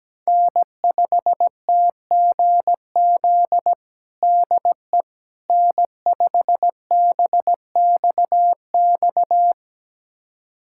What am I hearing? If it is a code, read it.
Morse code, heard as a keyed tone, about 17 wpm, 700 Hz. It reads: N5TGZ DE N5BXX
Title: USA Callsign DE Callsign - 5 Characters - 17wpm